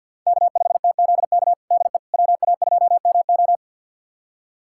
Morse code, heard as a keyed tone, about 52 wpm, 700 Hz. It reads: K5TZK DE WA2MO
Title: USA Callsign DE Callsign - 5 Characters - 50wpm